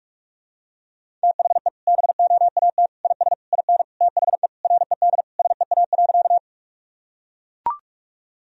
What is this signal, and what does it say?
Morse code, heard as a keyed tone, about 44 wpm, 700 Hz. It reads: THE BOAT IS IN THE RED SEA.
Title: Sentences from Top 1000 Words - Review Part 2 - 45wpm